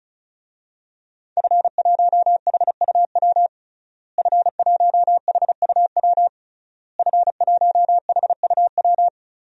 Morse code, heard as a keyed tone, about 34 wpm, 700 Hz. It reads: F1HUW F1HUW F1HUW
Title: DXCC-100 Call Signs - 5 Characters - Repeated 3x - 35wpm